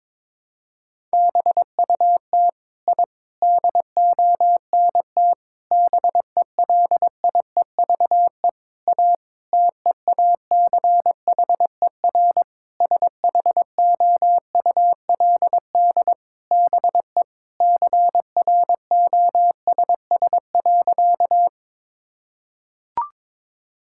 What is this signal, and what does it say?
Morse code, heard as a keyed tone, about 22 wpm, 700 Hz. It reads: BUT I DONT BELIEVE A TEACHER SHOULD BE CROSS.
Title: Sentences from Top 1000 Words - Review Part 6 - 22wpm